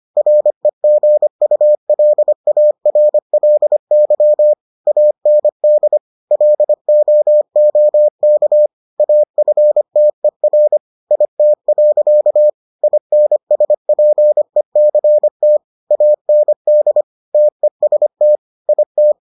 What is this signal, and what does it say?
Morse code, heard as a keyed tone, about 25 wpm, 600 Hz. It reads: REGULARLY AND LOOK AFTER IT. INSPECT AND TEST IT